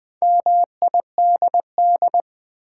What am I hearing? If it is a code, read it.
Morse code, heard as a keyed tone, about 20 wpm, 700 Hz. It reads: MIDD
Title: Build A Word - 2 Letters At A Time - 6-Letter Words - 20wpm